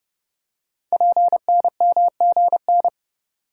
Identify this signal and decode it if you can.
Morse code, heard as a keyed tone, about 30 wpm, 690 Hz. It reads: PNMGN
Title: Code Groups - 5 Letters - 30wpm